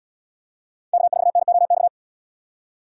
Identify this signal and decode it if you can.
Morse code, heard as a keyed tone, about 79 wpm, 700 Hz. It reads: K4NOX